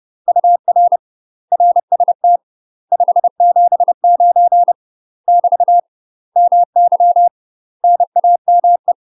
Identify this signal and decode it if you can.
Morse code, heard as a keyed tone, about 30 wpm, 700 Hz. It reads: UR RST 579 <BT> MY NAME
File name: UK MASTER QSOs TRAINING / 30WPM / UK MASTER QSOs 0017